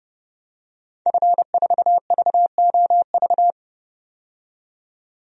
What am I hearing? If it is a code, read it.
Morse code, heard as a keyed tone, about 30 wpm, 700 Hz. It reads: F4VOV
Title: DXCC-100 Call Signs - 5 Characters - 30wpm